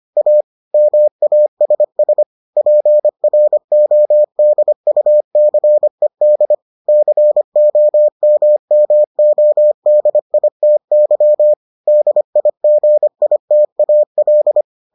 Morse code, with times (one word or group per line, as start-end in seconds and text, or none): 0.2-0.4 A
0.7-2.2 MASS
2.6-6.5 PRODUCED
6.9-11.5 COMMODITY
11.9-14.6 DIGITAL